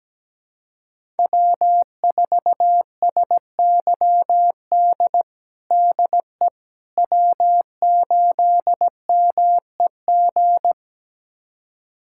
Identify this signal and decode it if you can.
Morse code, heard as a keyed tone, about 17 wpm, 700 Hz. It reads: W4SYD DE W8MEG
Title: USA Callsign DE Callsign - 5 Characters - 17wpm